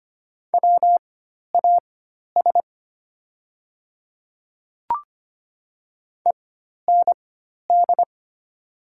Morse code, heard as a keyed tone, about 25 wpm, spaced as for 12 wpm, 700 Hz. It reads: WAS END
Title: 4x Character Spacing - Top 200 Words - 25wpm